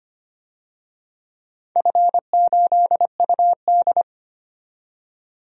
Morse code, heard as a keyed tone, about 25 wpm, 700 Hz. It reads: F8UD
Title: DXCC-100 Call Signs - 4 Characters - 25wpm